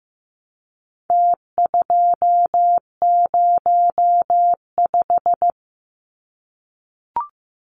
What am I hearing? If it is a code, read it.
Morse code, heard as a keyed tone, about 15 wpm, 700 Hz. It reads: T205
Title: Numbers - Contest Style - 4-Digit 15wpm